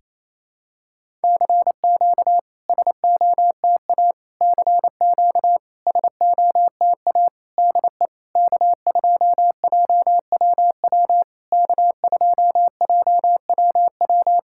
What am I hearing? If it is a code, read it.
Morse code, heard as a keyed tone, about 28 wpm, 700 Hz. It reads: CQ SOTA CQ SOTA DE K2JWW K2JWW
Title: SOTA QSO Practice 28wpm